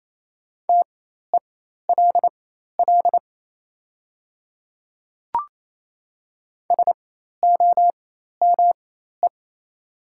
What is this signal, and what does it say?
Morse code, heard as a keyed tone, about 28 wpm, spaced as for 13 wpm, 690 Hz. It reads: TELL SOME
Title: 4x Character Spacing - Top 200 Words - 28wpm